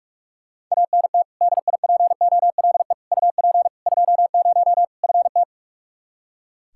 Morse code, heard as a keyed tone, about 42 wpm, 700 Hz. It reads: ANT DIPOLE UP 20 FT
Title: Practice QSO - Exchange 2 - Info - 45wpm